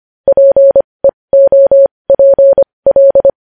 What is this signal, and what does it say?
Morse code, heard as a keyed tone, about 25 wpm, spaced as for 20 wpm, 550 Hz. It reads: PEOPL
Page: news-headlines-in-morse-code-at-20-wpm